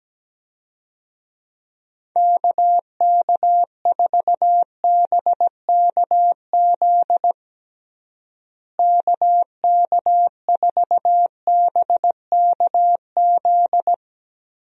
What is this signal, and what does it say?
Morse code, heard as a keyed tone, about 17 wpm, 700 Hz. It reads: KK4BKZ KK4BKZ
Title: USA Call Signs - 6 Characters - Repeated 3x - 17wpm